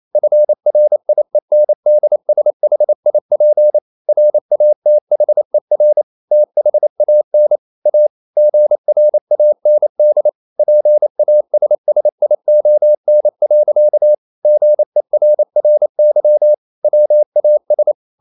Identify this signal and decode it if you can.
Morse code, heard as a keyed tone, about 28 wpm, 600 Hz. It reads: FRIENDSHIP RATHER THAN A GRAND PASSION. GERRY WAS
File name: Never 0027wpm